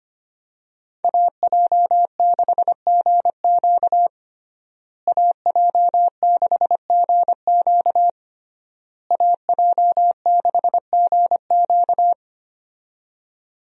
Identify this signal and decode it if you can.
Morse code, heard as a keyed tone, about 25 wpm, 700 Hz. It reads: AJ6GQ AJ6GQ AJ6GQ
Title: USA Call Signs - 5 Characters - Repeated 3x - 25wpm